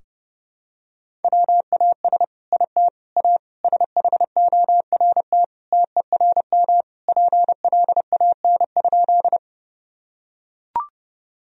Morse code, heard as a keyed tone, about 30 wpm, 700 Hz. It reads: WAS IT A SHORT TERM PLAN?
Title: Sentences from Top 1000 Words - Review Part 4 - 30wpm